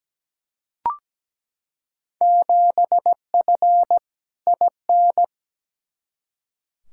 Morse code, heard as a keyed tone, about 17 wpm, 700 Hz. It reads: 7F IN
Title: Field Day Exchanges Encore - 17wpm